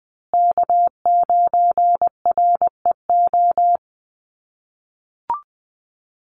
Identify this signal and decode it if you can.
Morse code, heard as a keyed tone, about 20 wpm, 700 Hz. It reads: K9REO
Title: USA Call Signs Repeated 3x 20wpm